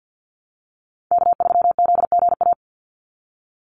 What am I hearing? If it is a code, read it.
Morse code, heard as a keyed tone, about 48 wpm, 700 Hz. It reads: K3ZGA
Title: Call Signs Repeated 3x - 50wpm